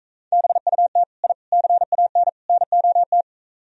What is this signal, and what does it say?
Morse code, heard as a keyed tone, about 43 wpm, 690 Hz. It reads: BUT I CAN NOT